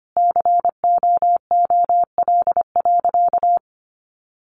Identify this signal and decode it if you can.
Morse code, heard as a keyed tone, about 25 wpm, 700 Hz. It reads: COOL.
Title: Sentences from Top 1200 Words 25wpm